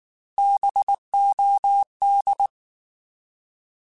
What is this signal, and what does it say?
Morse code, heard as a keyed tone, about 19 wpm, 800 Hz. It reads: BOD
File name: Alphabetical / Alphabetical mp3s / Bs @18